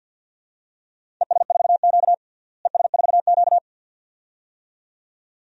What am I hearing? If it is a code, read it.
Morse code, heard as a keyed tone, about 52 wpm, 700 Hz. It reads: ES4Q ES4Q
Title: DXCC-100 Call Signs - 4 Characters - Repeated 2x - 50wpm